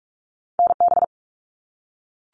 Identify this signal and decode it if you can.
Morse code, heard as a keyed tone, about 44 wpm, 700 Hz. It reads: NB
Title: Canadian Provinces and Territories - Repeated 3x - 45wpm